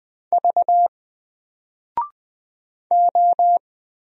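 Morse code, heard as a keyed tone, about 20 wpm, 700 Hz. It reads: V O
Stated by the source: Single letter and number Morse drill at 20 wpm, from N4LCD.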